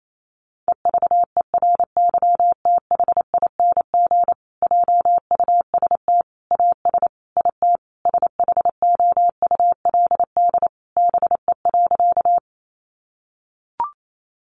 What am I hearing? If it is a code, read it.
Morse code, heard as a keyed tone, about 28 wpm, 700 Hz. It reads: EVERYTHING JUST AS IT SHOULD BE.
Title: Sentences from Top 700 Words 28wpm